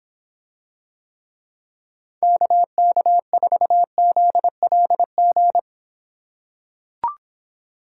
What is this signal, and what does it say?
Morse code, heard as a keyed tone, about 24 wpm, 700 Hz. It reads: KK4ZLG